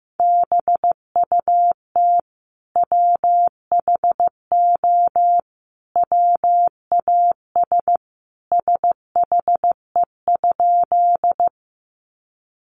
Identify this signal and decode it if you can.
Morse code, heard as a keyed tone, about 15 wpm, 700 Hz. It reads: BUT WHO WAS SHE?
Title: Sentences from Top 100 Words 15wpm